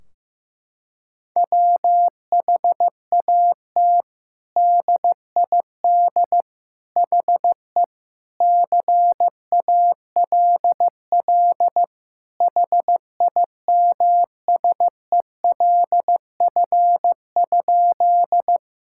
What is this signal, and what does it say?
Morse code, heard as a keyed tone, about 16 wpm, 700 Hz. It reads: WHAT DID HE CALL HIMSELF?
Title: Sentences from Top 1600 Words 15wpm